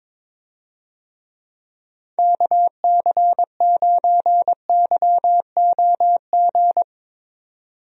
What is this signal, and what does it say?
Morse code, heard as a keyed tone, about 22 wpm, 700 Hz. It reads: KC9YOG
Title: USA Call Signs - 6 Characters - 22wpm